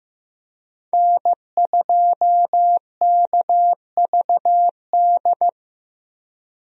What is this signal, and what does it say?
Morse code, heard as a keyed tone, about 15 wpm, 700 Hz. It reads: N2KVD
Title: USA Call Signs - 5 Characters - Repeated 2x - 15wpm